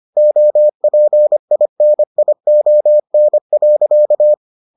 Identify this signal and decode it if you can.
Morse code, heard as a keyed tone, about 25 wpm, 600 Hz. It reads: OPINION.